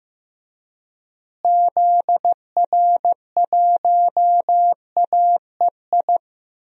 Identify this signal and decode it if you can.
Morse code, heard as a keyed tone, about 15 wpm, 700 Hz. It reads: ZR1AEI
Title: DXCC-100 Call Signs - 6 Characters - Repeated 3x - 15wpm